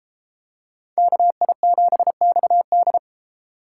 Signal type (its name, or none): Morse code